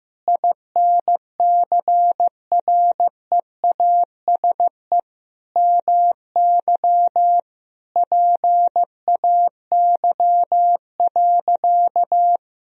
Morse code, heard as a keyed tone, about 15 wpm, 700 Hz. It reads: INCREASE MY PAY.
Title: Sentences from Top 1000 Words - Review Part 4 - 15wpm